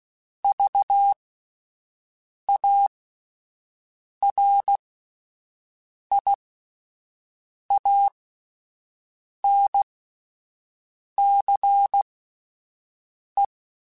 Morse code, heard as a keyed tone, about 16 wpm, spaced as for 5 wpm, 800 Hz. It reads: VARIANCE